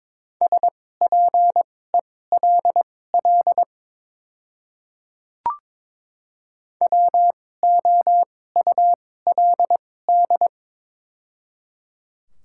Morse code, heard as a keyed tone, about 22 wpm, spaced as for 16 wpm, 700 Hz. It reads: SPELL WOULD
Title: 2x Character Spacing - Top 200 Words - 22wpm